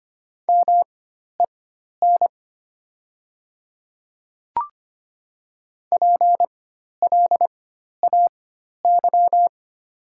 Morse code, heard as a keyed tone, about 25 wpm, spaced as for 12 wpm, 700 Hz. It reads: MEN PLAY
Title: 4x Character Spacing - Top 200 Words - 25wpm